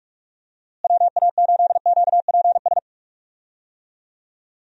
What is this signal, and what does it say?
Morse code, heard as a keyed tone, about 46 wpm, 700 Hz. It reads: WA8QPS
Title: USA Call Signs - 6 Characters - Repeated 3x - 45wpm